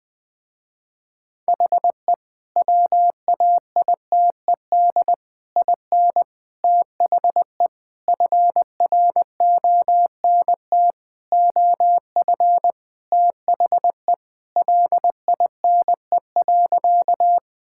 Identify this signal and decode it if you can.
Morse code, heard as a keyed tone, about 20 wpm, 700 Hz. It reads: HE WAITED IN THE FRONT OF THE LINE.
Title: Sentences with 4-Word Phrases - Spoken Beginning - 20wpm